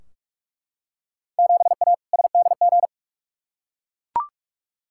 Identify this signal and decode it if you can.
Morse code, heard as a keyed tone, about 42 wpm, 700 Hz. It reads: 7A SDG
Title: Field Day Exchanges Encore - 45wpm